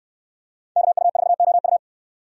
Morse code, heard as a keyed tone, about 64 wpm, 700 Hz. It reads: KU4OU